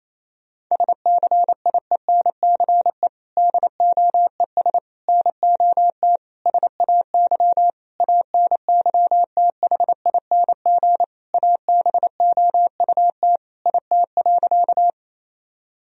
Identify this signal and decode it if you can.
Morse code, heard as a keyed tone, about 28 wpm, 700 Hz. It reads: SCIENCE DOES NOT SAY ANYTHING ABOUT IT.
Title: Sentences from Top 1000 Words - Review Part 2 - 28wpm